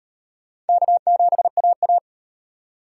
Morse code, heard as a keyed tone, about 33 wpm, 700 Hz. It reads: K7AA